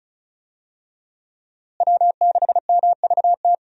Morse code, heard as a keyed tone, about 36 wpm, 700 Hz. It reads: W6MVT
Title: USA Call Signs - 5 Characters - Repeated 2x - 35wpm